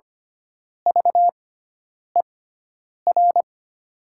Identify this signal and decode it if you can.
Morse code, heard as a keyed tone, about 25 wpm, spaced as for 9 wpm, 700 Hz. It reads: VER
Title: 6x Character Spacing - Top 500 Words - 25wpm